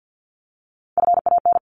Morse code, heard as a keyed tone, about 52 wpm, 700 Hz. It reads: FAN
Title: Top 1001-1250 Words 50wpm